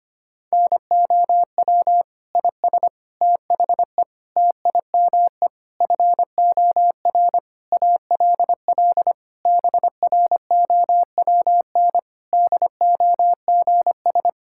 Morse code, heard as a keyed tone, about 25 wpm, 700 Hz. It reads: NOW IS THE TIME FOR ALL BROWN DOGS
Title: Pangram Challenge v2 - 25wpm